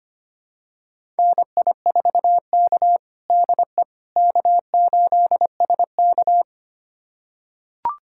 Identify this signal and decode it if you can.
Morse code, heard as a keyed tone, about 25 wpm, 700 Hz. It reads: NI4K DE K8SK
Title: USA Callsign DE Callsign - 4 Characters - 25wpm